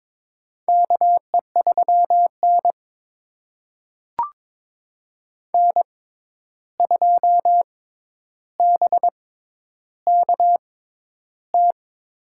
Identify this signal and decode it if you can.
Morse code, heard as a keyed tone, about 21 wpm, spaced as for 10 wpm, 700 Hz. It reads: KE3N N2BKT
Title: USA Call Signs - 6x Character Spacing - 22wpm